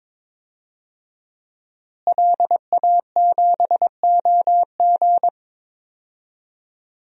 Morse code, heard as a keyed tone, about 21 wpm, 700 Hz. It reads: LA7OG
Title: DXCC-100 Call Signs - 5 Characters - Repeated 3x - 22wpm